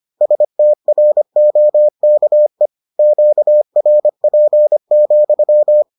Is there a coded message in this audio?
Yes: Morse code from 0.2 to 5.8 s, heard as a keyed tone, about 25 wpm, 600 Hz. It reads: STROKE QRP,